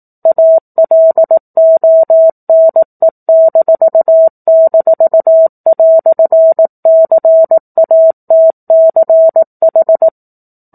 Morse code, heard as a keyed tone, about 18 wpm, 650 Hz. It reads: ALONE--"CATCH
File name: 04-Chapter-04 18 WPM